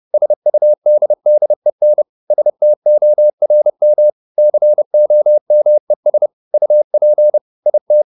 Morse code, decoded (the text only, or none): SUDDEN STORM COMES UP IT